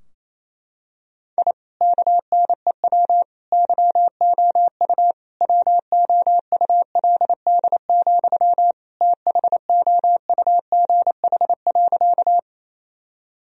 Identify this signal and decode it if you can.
Morse code, heard as a keyed tone, about 28 wpm, 700 Hz. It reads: I KNEW YOU WOULD, THOUGH.
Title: Sentences from Top 1000 Words - Review Part 2 - 28wpm